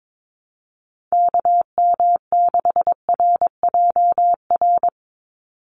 Morse code, heard as a keyed tone, about 22 wpm, 700 Hz. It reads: KM6RJR
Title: USA Call Signs 22wpm